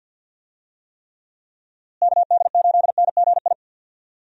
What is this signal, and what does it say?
Morse code, heard as a keyed tone, about 46 wpm, 700 Hz. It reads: KD8NGI
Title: USA Call Signs - 6 Characters - Repeated 2x - 50wpm